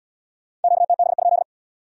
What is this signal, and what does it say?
Morse code, heard as a keyed tone, about 67 wpm, 700 Hz. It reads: QTB?